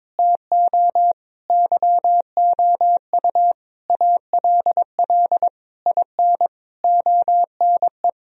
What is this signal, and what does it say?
Morse code, heard as a keyed tone, about 22 wpm, 700 Hz. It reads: TO YOU ALL IN ONE
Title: Sentences from Top 1000 Words - Review Part 1 - 22wpm